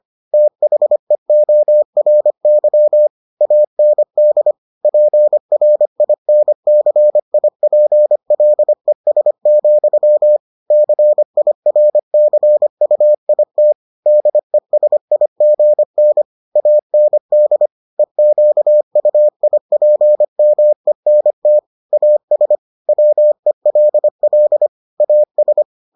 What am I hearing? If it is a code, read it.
Morse code, heard as a keyed tone, about 25 wpm, 600 Hz. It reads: THEORY AND PRINCIPLES, CIRCUIT DESIGN AND EQUIPMENT AS WELL AS